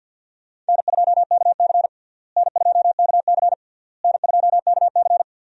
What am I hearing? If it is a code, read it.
Morse code, heard as a keyed tone, about 52 wpm, 700 Hz. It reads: N2KC N2KC N2KC
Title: USA Call Signs - 4 Characters - Repeated 3x - 50wpm